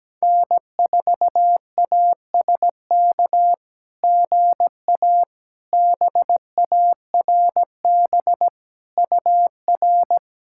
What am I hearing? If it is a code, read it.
Morse code, heard as a keyed tone, about 17 wpm, 700 Hz. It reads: N4ASK GA BARB UR
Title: SOTA QSO Practice 17wpm